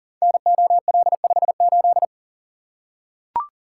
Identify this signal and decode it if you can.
Morse code, heard as a keyed tone, about 40 wpm, 700 Hz. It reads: NOL58
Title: Code Groups - 5 Letter:Numbers - 40wpm